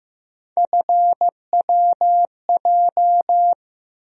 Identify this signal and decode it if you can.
Morse code, heard as a keyed tone, about 15 wpm, 700 Hz. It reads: FWJ
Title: USA Call Sign Postfixes - 3 Letters 15wpm